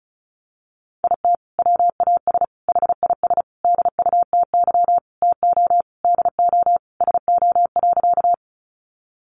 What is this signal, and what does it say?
Morse code, heard as a keyed tone, about 34 wpm, 700 Hz. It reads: IT WAS HIS DUTY TO DO SO.
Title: Sentences from Top 1500 Words 35wpm